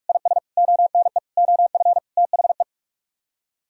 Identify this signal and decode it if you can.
Morse code, heard as a keyed tone, about 49 wpm, 700 Hz. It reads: IS ONE OF THE